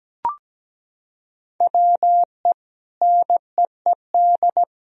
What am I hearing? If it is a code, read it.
Morse code, heard as a keyed tone, about 17 wpm, 700 Hz. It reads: WE NEED
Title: Sentences from Top 1000 Words - Review Part 4 - 17wpm